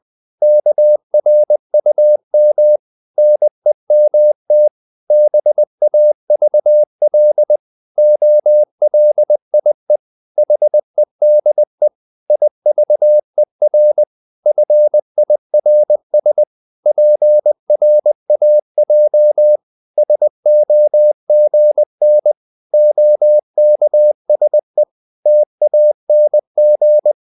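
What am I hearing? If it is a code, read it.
Morse code, heard as a keyed tone, about 20 wpm, 600 Hz. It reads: KRUM NEMT BAVL OLIE HEDE IVER FIRS PRAJ SOGN OKSE TANG
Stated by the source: Morse code practice of short Danish words at 20 wpm, from CW med Gnister.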